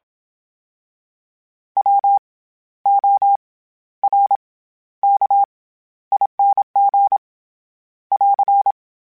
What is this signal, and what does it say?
Morse code, heard as a keyed tone, about 27 wpm, 800 Hz. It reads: W O R K ING <AR>
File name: Special INGs-mp3s / Special INGs-mp3s / SlowToFaster ing @25